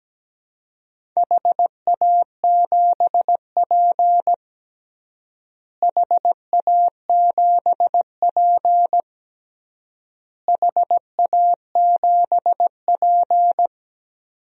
Morse code, heard as a keyed tone, about 17 wpm, 700 Hz. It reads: HA7P HA7P HA7P
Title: DXCC-100 Call Signs - 4 Characters - Repeated 3x - 17wpm